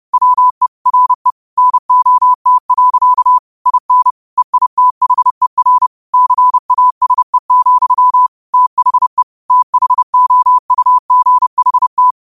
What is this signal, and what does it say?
Morse code, heard as a keyed tone, about 30 wpm, 1000 Hz. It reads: WE RE NOT. IN EITHER CASE, THE THOUGHT
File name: qotdfiles / 30wpm